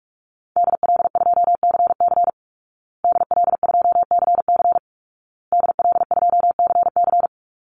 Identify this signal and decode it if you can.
Morse code, heard as a keyed tone, about 46 wpm, 700 Hz. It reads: DL2CC DL2CC DL2CC
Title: Call Signs Repeated 2x - 45wpm